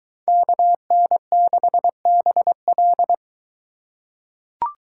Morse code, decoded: KN6BL

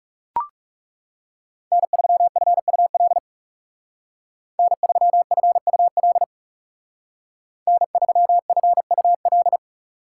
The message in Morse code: N3FUL N3FUL N3FUL

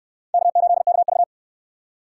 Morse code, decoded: K8KV

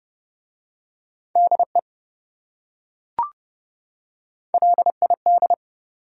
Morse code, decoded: DE LID

